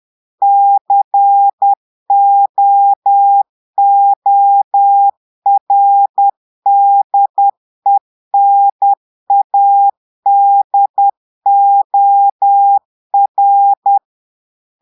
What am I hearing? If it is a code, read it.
Morse code, heard as a keyed tone, about 10 wpm, 800 Hz. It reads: COORDENADOR
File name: Portugais - texte en clair 1 - vitesse - 10 mots minute